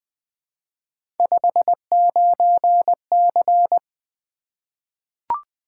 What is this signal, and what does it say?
Morse code, heard as a keyed tone, about 20 wpm, 700 Hz. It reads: 59C
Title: Three Letter-Number Combinations - 20wpm